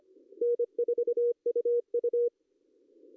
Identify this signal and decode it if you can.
Morse code, heard as a keyed tone, about 25 wpm, 480 Hz. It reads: N4UU
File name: CW / cqww-cw-k5tr